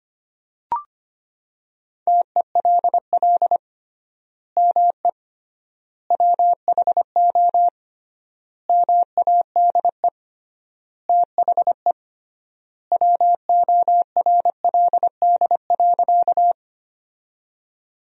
Morse code, heard as a keyed tone, about 25 wpm, 700 Hz. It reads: TELL ME WHO MADE THE WORLD.
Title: 3x Word-Spacing - Sentences from Top 200 Words - 25wpm